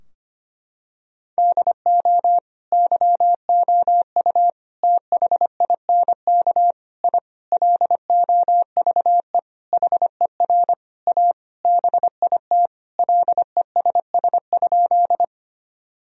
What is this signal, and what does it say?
Morse code, heard as a keyed tone, about 25 wpm, 700 Hz. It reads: DO YOU THINK I LOVE HER A BIT LESS?
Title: Sentences from Top 1000 Words - Review Part 2 - 25wpm